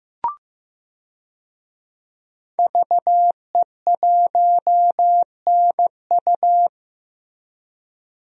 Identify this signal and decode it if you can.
Morse code, heard as a keyed tone, about 16 wpm, 700 Hz. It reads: VE1NU
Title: DXCC-100 Call Signs - 5 Characters - Repeated 2x - 15wpm